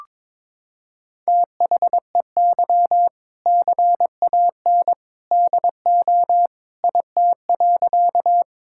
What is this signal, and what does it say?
Morse code, heard as a keyed tone, about 21 wpm, 700 Hz. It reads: THEY CAN DO IT.
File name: practice / 22wpm / Sentences with 2-Word Phrases - 22wpm